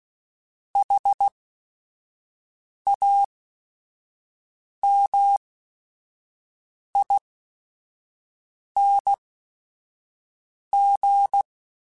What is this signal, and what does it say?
Morse code, heard as a keyed tone, about 16 wpm, spaced as for 5 wpm, 800 Hz. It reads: HAMING
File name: Special INGs / Special INGs-mp3s / Sngl Ing @5